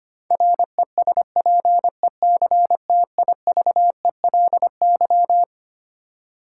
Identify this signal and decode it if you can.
Morse code, heard as a keyed tone, about 25 wpm, 700 Hz. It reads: RESPECTIVELY
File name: Twelve Letter Words - 25wpm